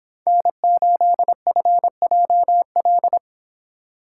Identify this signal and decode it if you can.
Morse code, heard as a keyed tone, about 27 wpm, 700 Hz. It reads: N8FJL
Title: Speed Racing - USA 1x3 Call Signs 22wpm